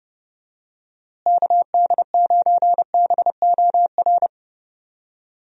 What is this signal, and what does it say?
Morse code, heard as a keyed tone, about 30 wpm, 700 Hz. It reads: KD9BOR
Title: USA Call Signs - 6 Characters - Repeated 3x - 30wpm